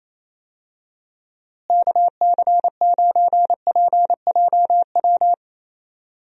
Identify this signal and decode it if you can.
Morse code, heard as a keyed tone, about 27 wpm, 700 Hz. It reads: KC9PJW
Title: USA Call Signs - 6 Characters - Repeated 3x - 28wpm